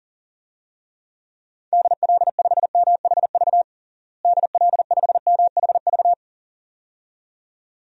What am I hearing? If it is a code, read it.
Morse code, heard as a keyed tone, about 37 wpm, 700 Hz. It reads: DL5MHV DL5MHV
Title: DXCC-100 Call Signs - 6 Characters - Repeated 2x - 40wpm